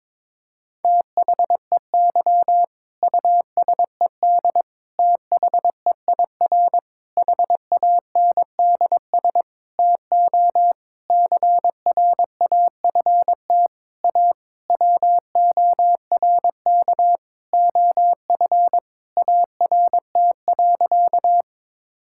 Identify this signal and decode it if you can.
Morse code, heard as a keyed tone, about 22 wpm, 700 Hz. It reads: THEY USED THEIR HANDS TO CRAFT A WORK OF ART.
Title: Sentences from Top 1000 Words - Review Part 6 - 22wpm